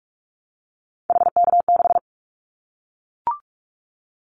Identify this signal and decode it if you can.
Morse code, heard as a keyed tone, about 44 wpm, 700 Hz. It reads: HK6
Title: Callsign Prefixes for Top 100 DXCC - 45wpm